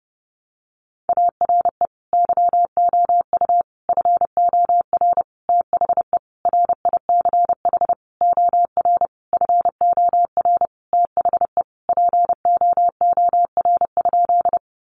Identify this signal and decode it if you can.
Morse code, heard as a keyed tone, about 30 wpm, 700 Hz. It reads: ARE YOU FOR THE RICH OR FOR THE POOR?
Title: Sentences from Top 1100 Words 30wpm